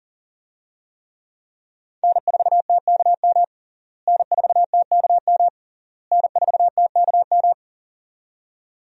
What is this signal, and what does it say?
Morse code, heard as a keyed tone, about 40 wpm, 700 Hz. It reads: N4TKM N4TKM N4TKM
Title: USA Call Signs - 5 Characters - Repeated 3x - 40wpm